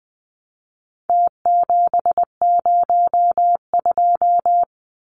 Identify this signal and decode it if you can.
Morse code, heard as a keyed tone, about 20 wpm, 700 Hz. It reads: T702
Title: Numbers - Contest Style - 4-Digit 20wpm